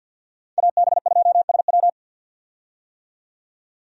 Morse code, heard as a keyed tone, about 52 wpm, 700 Hz. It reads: AB2SW
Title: USA Call Signs - 5 Characters - 50wpm